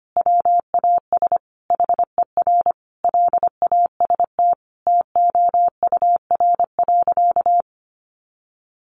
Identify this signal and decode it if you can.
Morse code, heard as a keyed tone, about 25 wpm, 700 Hz. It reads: WAS HER LAST TOUR.
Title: Sentences from Top 1400 Words 25wpm